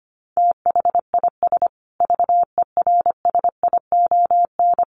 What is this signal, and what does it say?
Morse code, heard as a keyed tone, about 25 wpm, 700 Hz. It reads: THIS VERSION
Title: Sentences from Top 1400 Words 25wpm